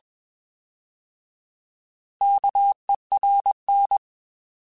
Morse code, heard as a keyed tone, about 21 wpm, 800 Hz. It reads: KERN
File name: Alphabetical / Alphabetical mp3s / Ks @20